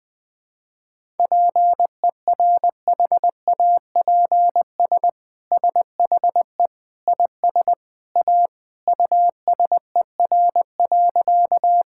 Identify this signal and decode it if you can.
Morse code, heard as a keyed tone, about 21 wpm, 700 Hz. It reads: PERHAPS SHE IS A USER.
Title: Sentences from Top 1000 Words - Review Part 5 - 20wpm